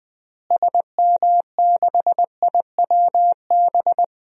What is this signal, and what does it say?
Morse code, heard as a keyed tone, about 21 wpm, 700 Hz. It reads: SM6IWB
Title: DXCC-100 Call Signs - 6 Characters - Repeated 2x - 20wpm